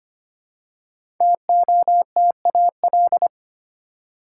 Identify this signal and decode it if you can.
Morse code, heard as a keyed tone, about 25 wpm, 700 Hz. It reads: TOTAL
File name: 5 Letter Words - First 3 Letters Given - 25wpm